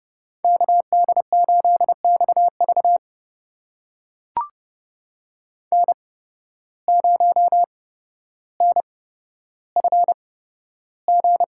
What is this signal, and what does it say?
Morse code, heard as a keyed tone, about 30 wpm, spaced as for 13 wpm, 700 Hz. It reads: KD8XV N0NFG
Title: USA Call Signs - 5 Characters - 8x Character Spacing - 30wpm